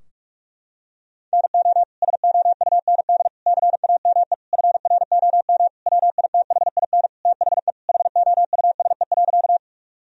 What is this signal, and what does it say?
Morse code, heard as a keyed tone, about 46 wpm, 700 Hz. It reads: NO SOUND CAME FROM WITHIN THE HOUSE.
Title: Sentences from Top 1000 Words - Review Part 4 - 45wpm